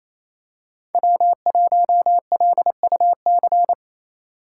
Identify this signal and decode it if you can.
Morse code, heard as a keyed tone, about 27 wpm, 700 Hz. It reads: W1LUC
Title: USA Call Signs - 5 Characters - 4x Character Spacing - 28wpm